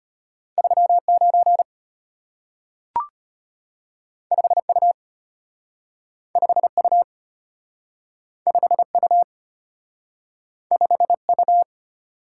Morse code, heard as a keyed tone, about 34 wpm, 700 Hz. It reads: 39 5U 5U 5U 5U